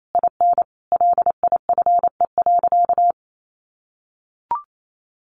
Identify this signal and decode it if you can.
Morse code, heard as a keyed tone, about 28 wpm, 700 Hz. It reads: IN LIFE.
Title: Sentences from Top 1200 Words 28wpm